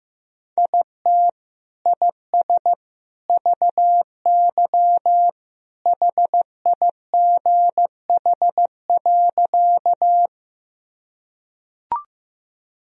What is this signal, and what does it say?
Morse code, heard as a keyed tone, about 15 wpm, 700 Hz. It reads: IT IS VY HIGH.
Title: Sentences from Top 200 Words and Abbreviations 15wpm